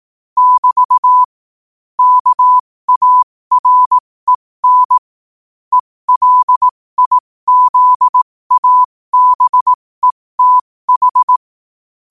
Morse code, heard as a keyed tone, about 18 wpm, spaced as for 15 wpm, 1000 Hz. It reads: <BT> KAREN ELIZABETH
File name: qotdfiles / 15wpm